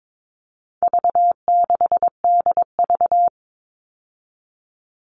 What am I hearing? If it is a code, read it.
Morse code, heard as a keyed tone, about 22 wpm, 700 Hz. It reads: V6DV
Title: All Cut Numbers - 4-Digit 22wpm